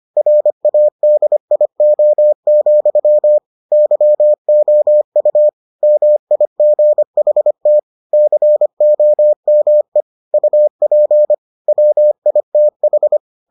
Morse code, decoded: RADIO, YOU MIGHT COME UP WITH

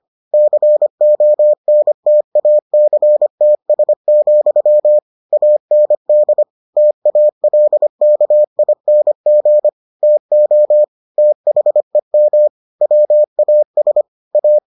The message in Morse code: CONTACTS, AND TALKING TO THEM WAS A

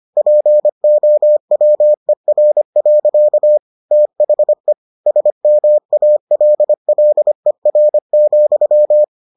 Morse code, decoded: POWER. THE SMALLER,

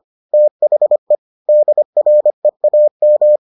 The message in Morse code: THE DREAM

